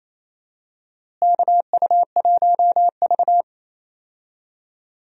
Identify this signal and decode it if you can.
Morse code, heard as a keyed tone, about 28 wpm, 700 Hz. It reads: KU1V
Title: USA Call Signs - 4 Characters - Repeated 3x - 28wpm